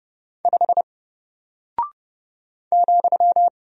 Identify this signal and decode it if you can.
Morse code, heard as a keyed tone, about 30 wpm, 700 Hz. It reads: 5 ,
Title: Single Character (Letter:Number:Punctuation:Prosign) Alt 30wpm